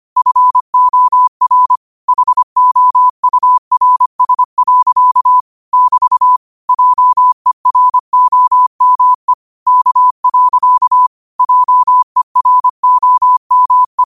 Morse code, heard as a keyed tone, about 25 wpm, 1000 Hz. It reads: FOR HOURS. <BT> JEROME K. JEROME